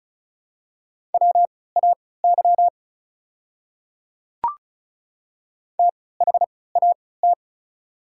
Morse code, heard as a keyed tone, about 36 wpm, spaced as for 20 wpm, 700 Hz. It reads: WAY THAT
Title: 3x Character Spacing - Top 300 Words - 35wpm